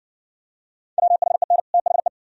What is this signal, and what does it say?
Morse code, heard as a keyed tone, about 60 wpm, 700 Hz. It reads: WHEN THE